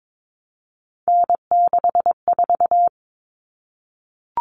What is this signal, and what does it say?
Morse code, heard as a keyed tone, about 22 wpm, 700 Hz. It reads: N64
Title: Common Cut Numbers - 3-Digit 22wpm